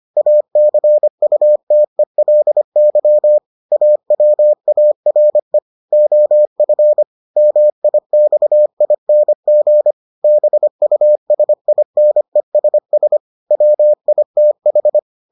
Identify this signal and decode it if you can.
Morse code, heard as a keyed tone, about 25 wpm, 600 Hz. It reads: ACUTELY AWARE OF MIXING BUSINESS WITH